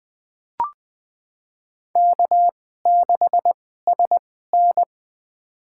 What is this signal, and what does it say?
Morse code, heard as a keyed tone, about 20 wpm, spaced as for 14 wpm, 700 Hz. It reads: K6SN
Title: USA Call Signs - 4 Characters - 2x Character Spacing - 20wpm